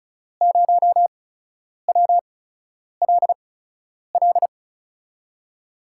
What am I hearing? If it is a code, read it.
Morse code, heard as a keyed tone, about 36 wpm, spaced as for 10 wpm, 700 Hz. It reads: 0WLL
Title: USA Call Signs - 5 Characters - 8x Character Spacing - 35wpm